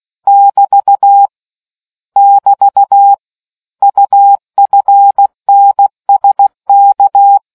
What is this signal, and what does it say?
Morse code, heard as a keyed tone, about 16 wpm, 800 Hz. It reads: <BT> <BT> UFNSK